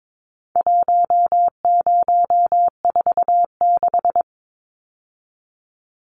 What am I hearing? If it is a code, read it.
Morse code, heard as a keyed tone, about 22 wpm, 700 Hz. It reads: 1046
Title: Numbers - 4-Digit 22wpm